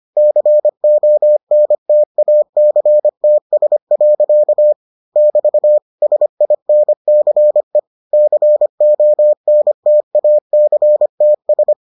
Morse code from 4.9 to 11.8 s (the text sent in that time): <BT> SINCE CONTACTS